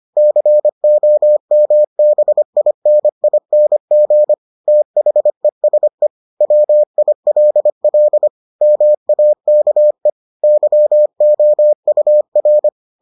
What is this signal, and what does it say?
Morse code, heard as a keyed tone, about 25 wpm, 600 Hz. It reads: COMBINING THESE WILL MAKE YOUR